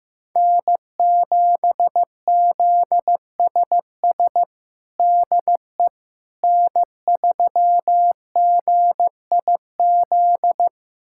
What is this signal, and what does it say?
Morse code, heard as a keyed tone, about 15 wpm, 700 Hz. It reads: N7ZSS DE N3GIZ
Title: USA Callsign DE Callsign - 5 Characters - 15wpm